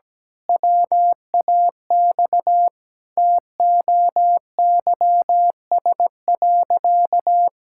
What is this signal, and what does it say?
Morse code, heard as a keyed tone, about 17 wpm, 700 Hz. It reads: WAX TOYS.
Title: Pangram Challenge v2 - 17wpm